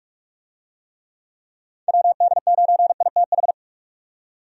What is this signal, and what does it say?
Morse code, heard as a keyed tone, about 46 wpm, 700 Hz. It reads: WD9ITH
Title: USA Call Signs - 6 Characters - 45wpm